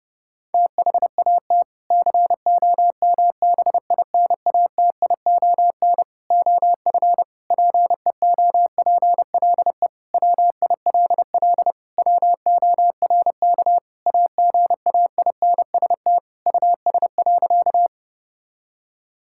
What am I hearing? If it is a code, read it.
Morse code, heard as a keyed tone, about 30 wpm, 700 Hz. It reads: THAT COMBINATION OF PEOPLE WILL WORK AGAINST US.
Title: Sentences from Top 1000 Words - Review Part 6 - 30wpm